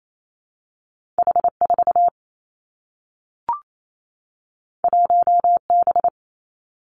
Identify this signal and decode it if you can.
Morse code, heard as a keyed tone, about 28 wpm, 700 Hz. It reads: H 4 1 B
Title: Two Challenging Letter-Number Combinations 28wpm